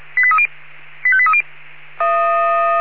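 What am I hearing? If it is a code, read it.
Five-tone selective call (ZVEI), heard as quick runs of five tones, tones of about 70 ms. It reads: 85830
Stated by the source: ZVEI five-tone selective call 85278, received by SDR.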